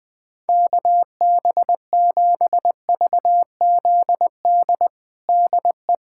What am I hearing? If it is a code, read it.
Morse code, heard as a keyed tone, about 21 wpm, 700 Hz. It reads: KB7VZD DE
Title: USA Callsign DE Callsign 20wpm